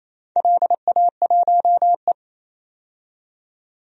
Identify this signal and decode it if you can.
Morse code, heard as a keyed tone, about 28 wpm, 700 Hz. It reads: LA1E
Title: DXCC-100 Call Signs - 4 Characters - 28wpm